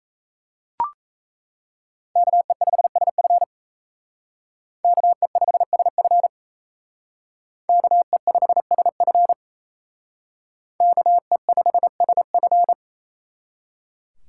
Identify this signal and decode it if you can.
Morse code, heard as a keyed tone, about 35 wpm, 700 Hz. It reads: KE5SF KE5SF KE5SF KE5SF